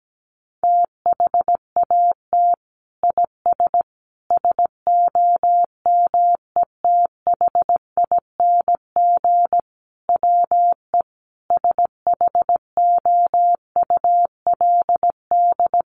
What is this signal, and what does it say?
Morse code, heard as a keyed tone, about 17 wpm, 700 Hz. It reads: THAT IS SOMETHING WE SHOULD
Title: Sentences from Top 1100 Words 17wpm